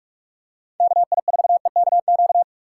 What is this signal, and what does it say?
Morse code, heard as a keyed tone, about 46 wpm, 700 Hz. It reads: KI4EKQ
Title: USA Call Signs - 4x Character Spacing - 45wpm